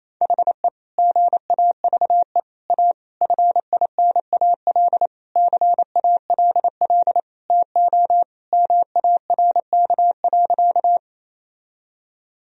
Morse code, heard as a keyed tone, about 28 wpm, 700 Hz. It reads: HE GAVE A FINAL CALL TO MARK.
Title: Sentences from Top 1000 Words - Review Part 2 - 28wpm